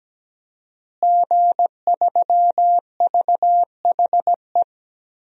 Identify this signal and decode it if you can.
Morse code, heard as a keyed tone, about 17 wpm, 700 Hz. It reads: G3VHE
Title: DXCC-100 Call Signs - 5 Characters - Repeated 2x - 17wpm